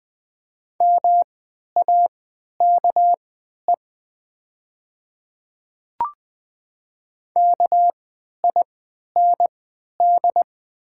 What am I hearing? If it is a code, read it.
Morse code, heard as a keyed tone, about 20 wpm, spaced as for 11 wpm, 700 Hz. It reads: MAKE KIND
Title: 3x Character Spacing - Top 300 Words - 20wpm